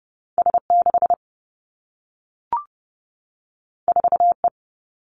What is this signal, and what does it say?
Morse code, heard as a keyed tone, about 30 wpm, 700 Hz. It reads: S6 4E
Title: Two Challenging Letter-Number Combinations 30wpm